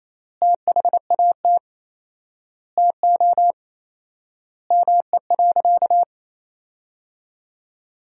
Morse code, heard as a keyed tone, about 27 wpm, 700 Hz. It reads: THAT TO ME.
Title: 4x Word-Spacing - Sentences from Top 400 Words - 28wpm